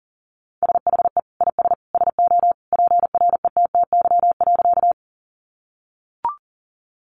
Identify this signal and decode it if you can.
Morse code, heard as a keyed tone, about 40 wpm, 700 Hz. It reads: SHE IS SO PRETTY.
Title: Sentences from Top 1000 Words 40wpm